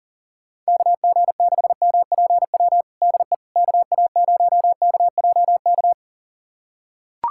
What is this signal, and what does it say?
Morse code, heard as a keyed tone, about 40 wpm, 700 Hz. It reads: KG6MPW DE KA0KJK
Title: USA Callsign DE Callsign - 6 Characters - 40wpm